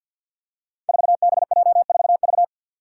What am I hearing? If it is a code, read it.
Morse code, heard as a keyed tone, about 51 wpm, 700 Hz. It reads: 4BJ4V
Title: Five Challenging Letter-Number Combinations 50wpm